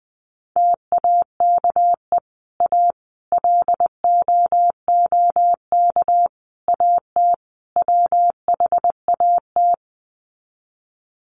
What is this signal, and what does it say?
Morse code, heard as a keyed tone, about 20 wpm, 700 Hz. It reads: TAKE A LOOK AT WHAT
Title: Sets of 5 Words Encore2 20wpm